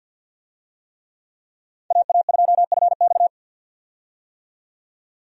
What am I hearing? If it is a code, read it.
Morse code, heard as a keyed tone, about 50 wpm, 700 Hz. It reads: AA2FX
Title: USA Call Signs - 5 Characters - 50wpm